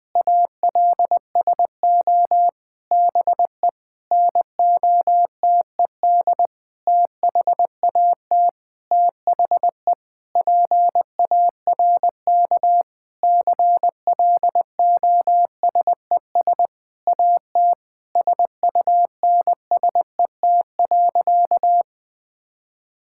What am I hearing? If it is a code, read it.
Morse code, heard as a keyed tone, about 20 wpm, 700 Hz. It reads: ALSO BE NOTED THAT THE PARK CLOSES AT SUNSET.
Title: Sentences with 6-Word Phrases - Spoken Beginning - 20wpm